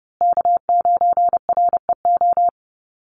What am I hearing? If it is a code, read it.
Morse code, heard as a keyed tone, about 30 wpm, 700 Hz. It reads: K9REO
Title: USA Call Signs Repeated 3x 30wpm